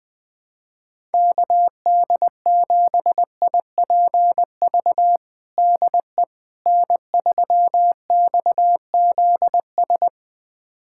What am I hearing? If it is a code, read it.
Morse code, heard as a keyed tone, about 20 wpm, 690 Hz. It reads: KD7IPV DE N3XZS
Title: USA Callsign DE Callsign 20wpm